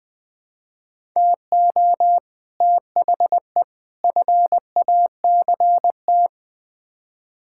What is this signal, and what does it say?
Morse code, heard as a keyed tone, about 21 wpm, 700 Hz. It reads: TO THE FACT
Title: Build A Phrase - 4-Word Phrases - 20wpm